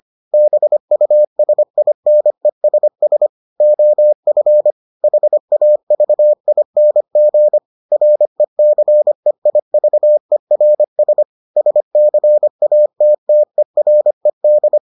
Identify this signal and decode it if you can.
Morse code, heard as a keyed tone, about 25 wpm, 600 Hz. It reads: BUSINESS OF HAVING RECEIVERS SCATTERED